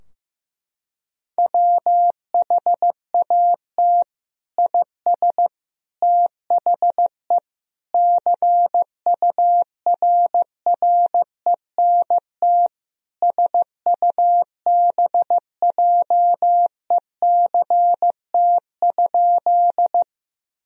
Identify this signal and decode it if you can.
Morse code, heard as a keyed tone, about 15 wpm, 700 Hz. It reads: WHAT IS THE CURRENT SUBJECT?
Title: Sentences from Top 1000 Words - Review Part 4 - 15wpm